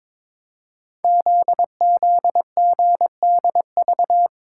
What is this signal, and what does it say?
Morse code, heard as a keyed tone, about 22 wpm, 700 Hz. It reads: ZZGDV
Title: Code Groups - 5 Letters - 22wpm